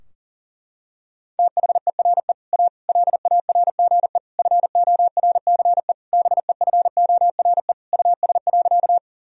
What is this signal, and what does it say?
Morse code, heard as a keyed tone, about 40 wpm, 700 Hz. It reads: THERE A LARGE FORCE BEFORE US.
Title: Sentences from Top 1000 Words - Review Part 3 - 40wpm